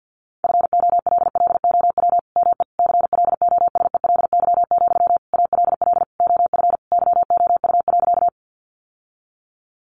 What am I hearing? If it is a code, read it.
Morse code, heard as a keyed tone, about 50 wpm, 700 Hz. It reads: FOLLOW ME CLOSELY, ALL OF YOU.
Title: Sentences from Top 1400 Words 50wpm